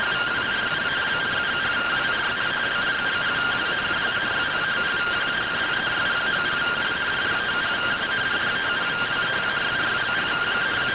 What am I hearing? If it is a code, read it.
Two-tone teleprinter signal, about 170 Hz shift.